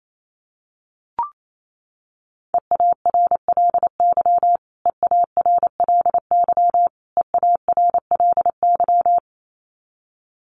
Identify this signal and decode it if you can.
Morse code, heard as a keyed tone, about 28 wpm, 700 Hz. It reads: EARLY EARLY EARLY
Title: Top 500 Words Repeated 28wpm